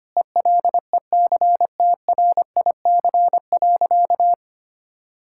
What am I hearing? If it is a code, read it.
Morse code, heard as a keyed tone, about 25 wpm, 690 Hz. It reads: ELECTRIC.
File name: practice / 25wpm / Difficult Words - Speed-Racing with Hints - 25wpm